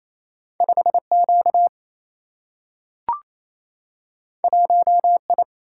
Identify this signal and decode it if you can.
Morse code, heard as a keyed tone, about 28 wpm, 700 Hz. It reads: 5 Q 1 I
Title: Single-Letter Single-Number 28wpm